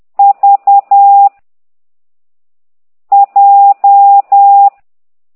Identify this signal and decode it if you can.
Morse code, heard as a keyed tone, about 10 wpm, 800 Hz. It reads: V J
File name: ct2idw / exerc9